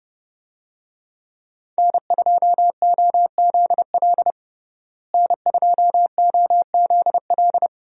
Morse code, heard as a keyed tone, about 32 wpm, 700 Hz. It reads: N2OZL N2OZL
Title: USA Call Signs - 5 Characters - Repeated 3x - 30wpm